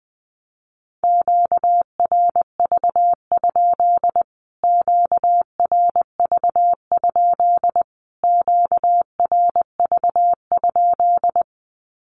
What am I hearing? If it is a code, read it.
Morse code, heard as a keyed tone, about 20 wpm, 700 Hz. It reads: QRV? QRV? QRV?
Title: Q-Codes Repeated 3x 20wpm